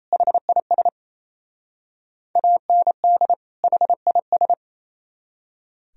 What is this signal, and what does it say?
Morse code, heard as a keyed tone, about 36 wpm, 700 Hz. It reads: HIS AND HIS